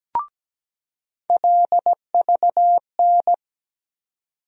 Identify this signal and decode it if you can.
Morse code, heard as a keyed tone, about 17 wpm, 700 Hz. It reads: LVN
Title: USA Call Sign Postfixes - 3 Letters 17wpm